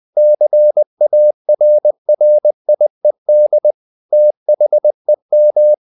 Morse code, heard as a keyed tone, about 20 wpm, 600 Hz. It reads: CARRIED THEM
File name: the old man and the sea / the old man and the sea part 1 20wpm